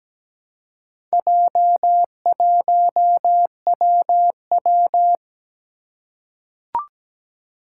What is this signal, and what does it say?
Morse code, heard as a keyed tone, about 17 wpm, 700 Hz. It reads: J1WW
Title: Four Challenging Letter-Number Combinations 17wpm